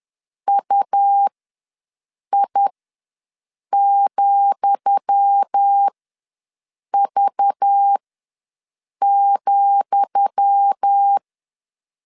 Morse code, heard as a keyed tone, about 10 wpm, spaced as for 6 wpm, 800 Hz. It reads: UI,V,
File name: LEZIONI / 10 SIGNS / 10D COMA ws 10wpm 4314